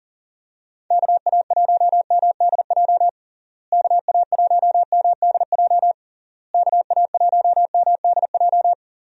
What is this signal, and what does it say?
Morse code, heard as a keyed tone, about 40 wpm, 700 Hz. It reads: KA1MDJ KA1MDJ KA1MDJ
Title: USA Call Signs - 6 Characters - Repeated 3x - 40wpm